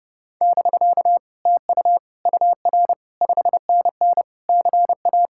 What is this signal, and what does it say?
Morse code, heard as a keyed tone, about 30 wpm, 700 Hz. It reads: <BK> TU UR 5NN CA